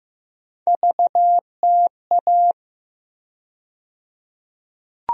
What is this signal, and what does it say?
Morse code, heard as a keyed tone, about 15 wpm, 700 Hz. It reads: VTA
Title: USA Call Sign Postfixes - 3 Letters 15wpm